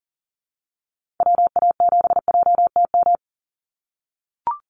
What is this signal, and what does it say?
Morse code, heard as a keyed tone, about 40 wpm, 700 Hz. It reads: WA7JTM
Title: NA SOTA Activator Callsigns 40wpm